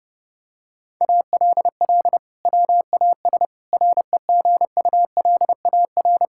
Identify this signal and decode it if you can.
Morse code, heard as a keyed tone, about 30 wpm, 700 Hz. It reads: ALL WAS REGULAR
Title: Sentences from Top 1000 Words - Review Part 6 - 30wpm